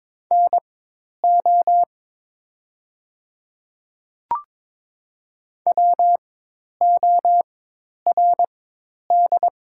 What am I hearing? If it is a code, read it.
Morse code, heard as a keyed tone, about 22 wpm, spaced as for 10 wpm, 700 Hz. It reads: NO WORD
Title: 4x Character Spacing - Top 100 Words - 22wpm